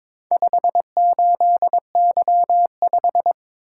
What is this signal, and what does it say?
Morse code, heard as a keyed tone, about 22 wpm, 700 Hz. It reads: 58Y5
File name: practice / 22wpm / Four Letter-Number Combinations - 22wpm